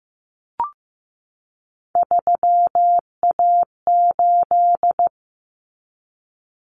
Morse code, heard as a keyed tone, about 15 wpm, 700 Hz. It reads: 3A8
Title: Common Cut Numbers - 3-Digit 15wpm